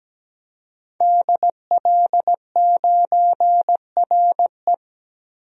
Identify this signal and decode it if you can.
Morse code, heard as a keyed tone, about 17 wpm, 700 Hz. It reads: DL9RE
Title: DXCC-100 Call Signs - 5 Characters - Repeated 2x - 17wpm